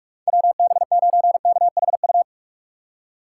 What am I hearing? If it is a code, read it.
Morse code, heard as a keyed tone, about 46 wpm, 700 Hz. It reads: WB9KHU
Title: USA Call Signs - 6 Characters - 3x Character Spacing - 45wpm